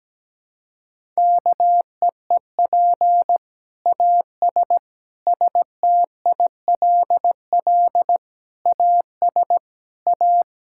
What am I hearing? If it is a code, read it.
Morse code, heard as a keyed tone, about 17 wpm, 700 Hz. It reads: KEEP AS STILL AS A
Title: Sentences from Top 1000 Words - Review Part 5 - 17wpm